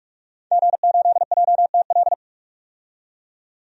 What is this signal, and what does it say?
Morse code, heard as a keyed tone, about 46 wpm, 700 Hz. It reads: G8JTL
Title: DXCC-100 Call Signs - 5 Characters - 45wpm